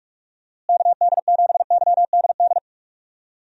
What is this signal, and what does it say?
Morse code, heard as a keyed tone, about 44 wpm, 700 Hz. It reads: KD7YDD
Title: USA Call Signs - 6 Characters - Repeated 2x - 45wpm